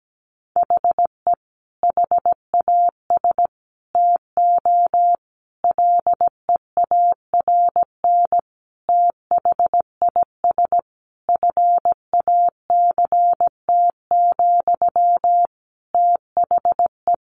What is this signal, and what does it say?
Morse code, heard as a keyed tone, about 17 wpm, 700 Hz. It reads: HE HAS TO LEARN THIS FACT, THE
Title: Sentences from Top 1200 Words 17wpm